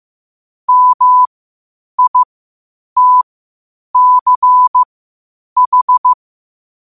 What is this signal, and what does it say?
Morse code, heard as a keyed tone, about 15 wpm, spaced as for 8 wpm, 1000 Hz. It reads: MITCH
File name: qotdfiles / 8wpm